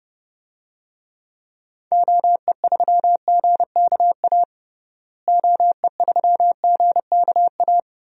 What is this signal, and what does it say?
Morse code, heard as a keyed tone, about 30 wpm, 700 Hz. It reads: OE3GKA OE3GKA
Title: DXCC-100 Call Signs - 6 Characters - Repeated 2x - 30wpm